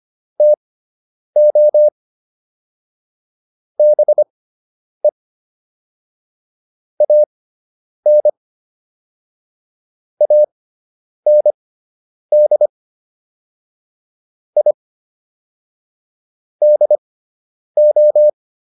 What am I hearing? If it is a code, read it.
Morse code, heard as a keyed tone, about 25 wpm, spaced as for 9 wpm, 600 Hz. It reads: TO BE AN AND I DO